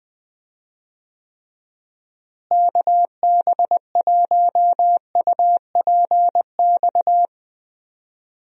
Morse code, heard as a keyed tone, about 20 wpm, 700 Hz. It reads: KB1UPX
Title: USA Call Signs - 6 Characters - Repeated 3x - 20wpm